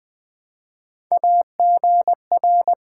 Morse code, heard as a keyed tone, about 20 wpm, 700 Hz. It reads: AGR
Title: Build A Word - 1 Letter At A Time - 3 to 5-Letter Words - 20wpm